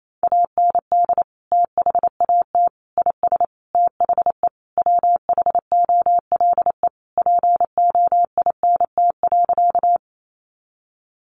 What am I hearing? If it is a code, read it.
Morse code, heard as a keyed tone, about 28 wpm, 700 Hz. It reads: AND THAT IS THE WHOLE POINT.
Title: Sentences from Top 400 Words 28wpm